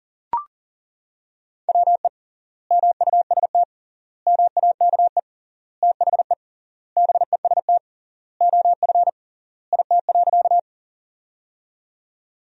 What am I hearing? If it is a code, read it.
Morse code, heard as a keyed tone, about 40 wpm, 700 Hz. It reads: WE MUST MAKE THE BEST OF IT.
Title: 3x Word-Spacing - Sentences from Top 400 Words - 40wpm